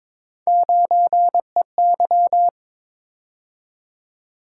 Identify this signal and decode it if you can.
Morse code, heard as a keyed tone, about 22 wpm, 700 Hz. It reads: 9EY
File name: practice / 22wpm / Three Letter-Number Combinations - 22wpm